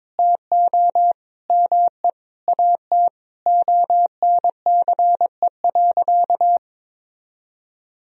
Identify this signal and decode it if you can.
Morse code, heard as a keyed tone, about 21 wpm, 700 Hz. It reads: TO ME AT ONCE.
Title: Sentences from Top 500 Words and Abbreviations 22wpm